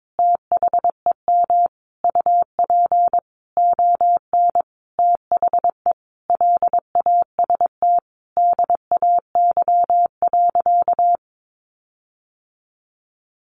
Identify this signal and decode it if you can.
Morse code, heard as a keyed tone, about 22 wpm, 700 Hz. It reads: THEM UP ON THE LAST DAY.
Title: Sentences from Top 1400 Words 22wpm